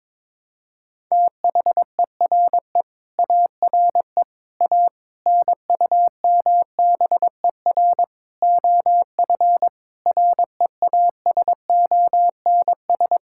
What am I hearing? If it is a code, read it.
Morse code, heard as a keyed tone, about 22 wpm, 700 Hz. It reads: THERE ARE A NUMBER OF REASONS
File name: practice / 22wpm / Sets of 8 Words - 22wpm